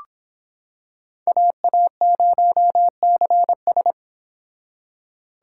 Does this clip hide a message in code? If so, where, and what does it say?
Yes: Morse code from 1.3 to 3.9 s, heard as a keyed tone, about 25 wpm, 700 Hz. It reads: AA0CS